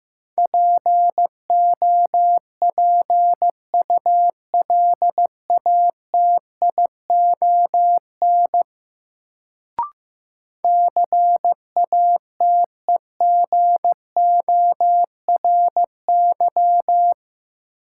Morse code, heard as a keyed tone, about 15 wpm, 700 Hz. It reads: POPULATION CATEGORY
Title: Sentences from Top 1000 Words - Review Part 7 - New Word Drill - 15wpm